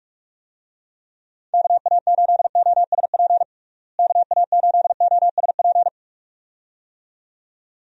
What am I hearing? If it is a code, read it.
Morse code, heard as a keyed tone, about 46 wpm, 700 Hz. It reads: KA8OSP KA8OSP
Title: USA Call Signs - 6 Characters - Repeated 2x - 45wpm